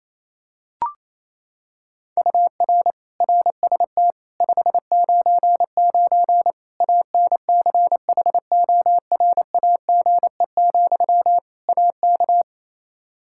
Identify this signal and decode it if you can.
Morse code, heard as a keyed tone, about 28 wpm, 700 Hz. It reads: UR RST 599 ANCHORAGE, AK
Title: K1USN SST QSO Practice 28wpm